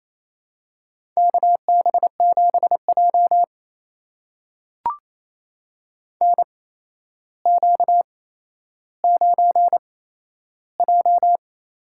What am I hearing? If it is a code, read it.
Morse code, heard as a keyed tone, about 27 wpm, spaced as for 12 wpm, 700 Hz. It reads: KB7J NQ9J
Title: USA Call Signs - 4 Characters - 8x Character Spacing - 28wpm